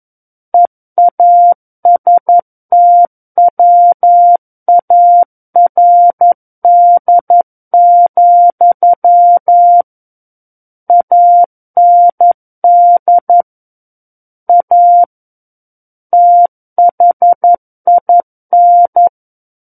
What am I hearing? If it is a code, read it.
Morse code, heard as a keyed tone, about 11 wpm, 700 Hz. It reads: EASTWARD, AND A THIN
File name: Book One / 02-Chapter 2 - 11 WPM